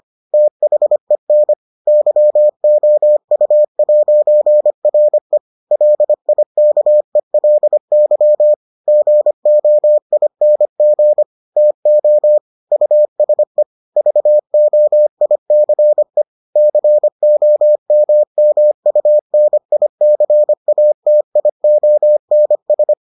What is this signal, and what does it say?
Morse code, heard as a keyed tone, about 25 wpm, 600 Hz. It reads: THEN YOU'RE LIKELY GOING TO USE VOICE COMMUNICATIONS